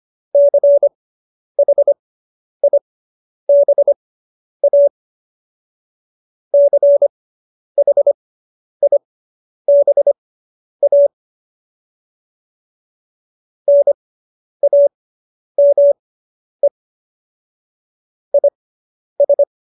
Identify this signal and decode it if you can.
Morse code, heard as a keyed tone, about 25 wpm, spaced as for 10 wpm, 570 Hz. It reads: CHIBA CHIBA NAME IS